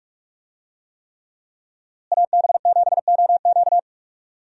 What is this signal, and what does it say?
Morse code, heard as a keyed tone, about 41 wpm, 700 Hz. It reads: AB7OQ
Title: USA Call Signs - 5 Characters - Repeated 3x - 45wpm